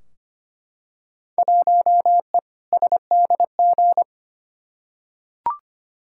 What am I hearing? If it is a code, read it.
Morse code, heard as a keyed tone, about 25 wpm, 700 Hz. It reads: 1E SDG
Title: Field Day Exchanges - 25wpm